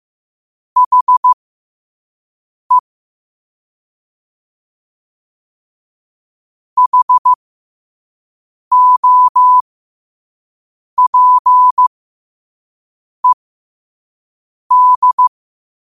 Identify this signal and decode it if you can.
Morse code, heard as a keyed tone, about 15 wpm, spaced as for 5 wpm, 1000 Hz. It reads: HE HOPED